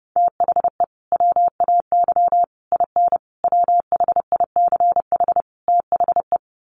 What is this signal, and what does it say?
Morse code, heard as a keyed tone, about 30 wpm, 700 Hz. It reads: THE WAY IN WHICH THE
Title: Sets of 5 Words Encore2 30wpm